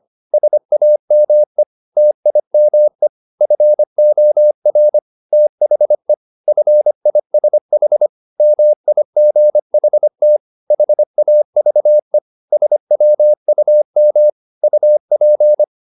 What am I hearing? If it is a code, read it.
Morse code, heard as a keyed tone, about 25 wpm, 600 Hz. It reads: SAME TIME FOR THE FISH MIGHT HAVE SWUM UP
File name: the old man and the sea part 1 25wpm